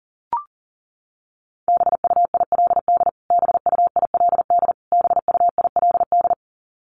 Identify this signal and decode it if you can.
Morse code, heard as a keyed tone, about 40 wpm, 700 Hz. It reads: BUILD BUILD BUILD
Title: Top 500 Words Repeated 40wpm